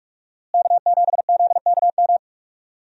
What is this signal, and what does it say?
Morse code, heard as a keyed tone, about 42 wpm, 700 Hz. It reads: K7ZKM
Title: USA Call Signs - 5 Characters - Repeated 2x - 45wpm